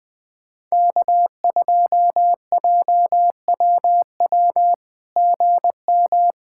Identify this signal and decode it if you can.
Morse code, heard as a keyed tone, about 20 wpm, 700 Hz. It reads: K2JWW GM
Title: SOTA QSO Practice 20wpm